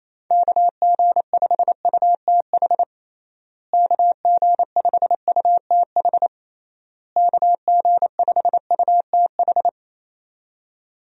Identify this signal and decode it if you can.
Morse code, heard as a keyed tone, about 28 wpm, 700 Hz. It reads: KG5UTH KG5UTH KG5UTH
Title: USA Call Signs - 6 Characters - Repeated 3x - 28wpm